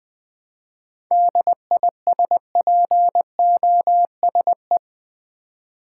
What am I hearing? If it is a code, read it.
Morse code, heard as a keyed tone, about 20 wpm, 700 Hz. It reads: DISPOSE
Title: 7 Letter Words - First 3 Letters Given - 20wpm